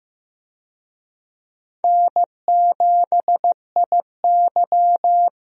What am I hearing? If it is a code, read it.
Morse code, heard as a keyed tone, about 15 wpm, 700 Hz. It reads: N7IY
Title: USA Call Signs - 4 Characters - 15wpm